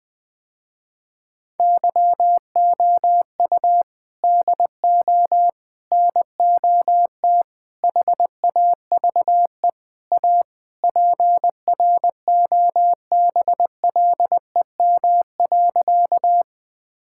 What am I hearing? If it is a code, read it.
Morse code, heard as a keyed tone, about 20 wpm, 700 Hz. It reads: YOU DO NOT HAVE A PROBLEM.
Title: Sentences with 5-Word Phrases - Spoken Beginning - 20wpm